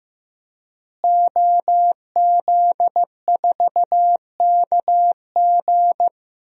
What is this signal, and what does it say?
Morse code, heard as a keyed tone, about 15 wpm, 700 Hz. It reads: OZ4KG
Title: DXCC-100 Call Signs - 5 Characters - Repeated 3x - 15wpm